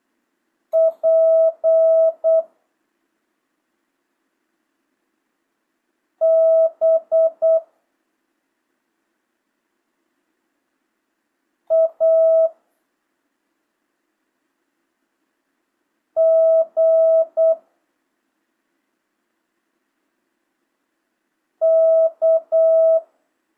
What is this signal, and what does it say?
Morse code, heard as a keyed tone, about 8 wpm, 650 Hz. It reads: P B A G K
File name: jh8cbh / 2025cw